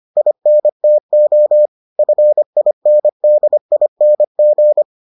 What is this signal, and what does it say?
Morse code, heard as a keyed tone, about 25 wpm, 600 Hz. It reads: INTO FINDING